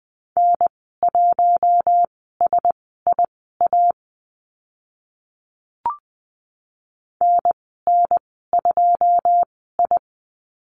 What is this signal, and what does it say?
Morse code, heard as a keyed tone, about 20 wpm, spaced as for 15 wpm, 700 Hz. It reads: N1SIA NN2I
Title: 088 Lesson 15g - Number 5 - Call Signs 20wpm